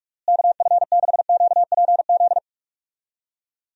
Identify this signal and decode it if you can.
Morse code, heard as a keyed tone, about 46 wpm, 700 Hz. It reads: KF6QPZ
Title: USA Call Signs - 6 Characters - Repeated 3x - 45wpm